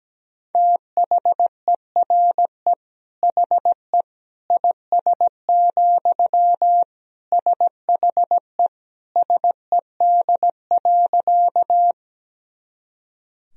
Morse code, heard as a keyed tone, about 17 wpm, 700 Hz. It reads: THERE HE IS, SHE SED.
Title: Sentences from Top 100 Words and Abbreviations 17wpm